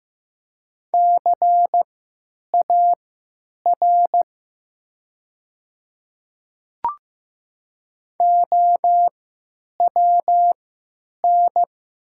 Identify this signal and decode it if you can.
Morse code, heard as a keyed tone, about 15 wpm, spaced as for 9 wpm, 700 Hz. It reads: CAR OWN
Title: 3x Character Spacing - Top 500 Words - 15wpm